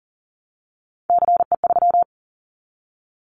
Morse code, heard as a keyed tone, about 40 wpm, 700 Hz. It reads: CE3
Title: Callsign Prefixes for Top 100 DXCC - 40wpm